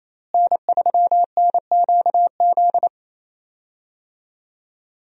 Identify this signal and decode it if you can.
Morse code, heard as a keyed tone, about 28 wpm, 700 Hz. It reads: N3NQZ
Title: USA Call Signs - 5 Characters - Repeated 3x - 28wpm